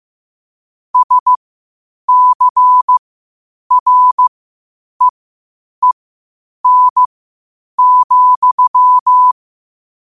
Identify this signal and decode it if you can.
Morse code, heard as a keyed tone, about 15 wpm, spaced as for 8 wpm, 1000 Hz. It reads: SCREEN,
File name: qotdfiles / 8wpm